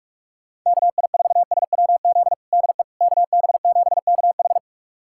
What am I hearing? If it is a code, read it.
Morse code, heard as a keyed tone, about 41 wpm, 700 Hz. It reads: KI4SWZ DE KB7KH
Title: USA Callsign DE Callsign 45wpm